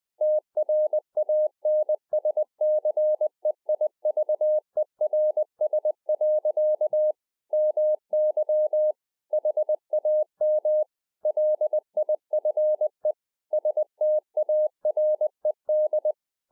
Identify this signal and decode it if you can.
Morse code, heard as a keyed tone, about 20 wpm, 610 Hz. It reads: TRANSCEIVERS. MY HAM LIFE STARED
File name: 2011-04-mp3 / A SMALL TRANCEIVER 3 20WPM